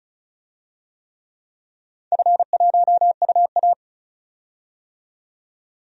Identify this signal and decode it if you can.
Morse code, heard as a keyed tone, about 36 wpm, 700 Hz. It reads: F1UA
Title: DXCC-100 Call Signs - 4 Characters - 35wpm